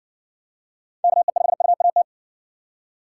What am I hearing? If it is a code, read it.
Morse code, heard as a keyed tone, about 54 wpm, 700 Hz. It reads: KE5UAT